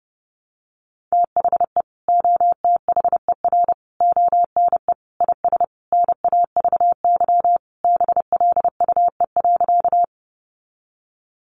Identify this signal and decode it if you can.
Morse code, heard as a keyed tone, about 30 wpm, 700 Hz. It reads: THE OTHER ONE IS NAVY BLUE.
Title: Additional Easy Phrases 30wpm